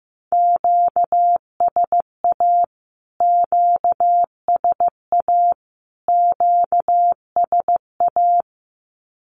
Morse code, heard as a keyed tone, about 15 wpm, 700 Hz. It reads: QSA QSA QSA
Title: Q-Codes Repeated 3x 15wpm